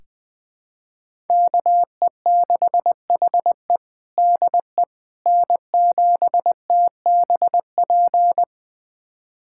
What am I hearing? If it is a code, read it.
Morse code, heard as a keyed tone, about 20 wpm, 700 Hz. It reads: KE6HE DE N7TBP
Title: USA Callsign DE Callsign - 5 Characters - 20wpm